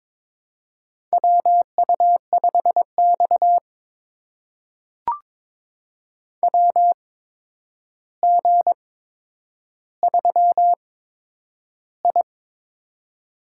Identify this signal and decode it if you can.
Morse code, heard as a keyed tone, about 22 wpm, spaced as for 9 wpm, 700 Hz. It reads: WU5X WG3I
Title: USA Call Signs - 4 Characters - 8x Character Spacing - 22wpm